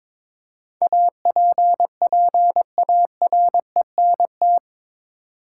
Morse code, heard as a keyed tone, about 22 wpm, 700 Hz. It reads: APPARENT
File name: practice / 22wpm / Eight Letter Words - 22wpm